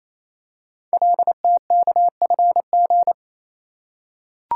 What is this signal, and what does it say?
Morse code, heard as a keyed tone, about 29 wpm, 700 Hz. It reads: LTKFG
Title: Code Groups - 5 Letters - 28wpm